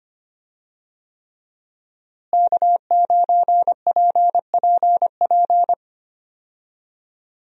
Morse code, heard as a keyed tone, about 25 wpm, 700 Hz. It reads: K9PPP
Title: USA Call Signs - 5 Characters - Repeated 3x - 25wpm